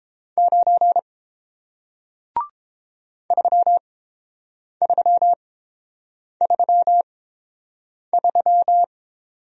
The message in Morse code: 9 3333

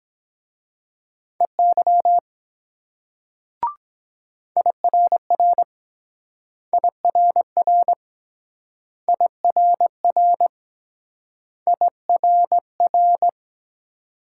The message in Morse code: EY IRR IRR IRR IRR

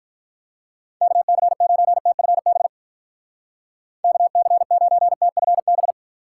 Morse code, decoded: KC9TFB KC9TFB

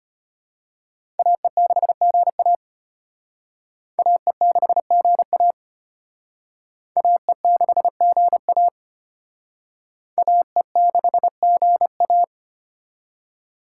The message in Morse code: AE6GA AE6GA AE6GA AE6GA

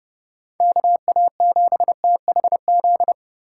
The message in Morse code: KA7THZ

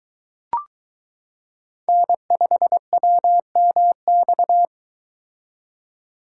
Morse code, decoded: N5WMX

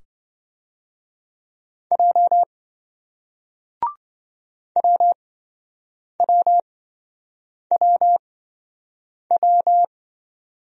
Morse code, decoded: J WWWW

